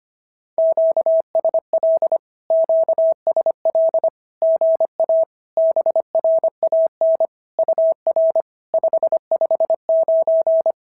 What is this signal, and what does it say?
Morse code, heard as a keyed tone, about 24 wpm, 650 Hz. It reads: QSL QSL GA BRAN UR 559
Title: SOTA QSO Practice 25wpm